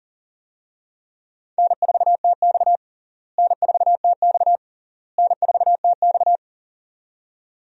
Morse code, heard as a keyed tone, about 40 wpm, 700 Hz. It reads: N4TX N4TX N4TX
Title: USA Call Signs - 4 Characters - Repeated 3x - 40wpm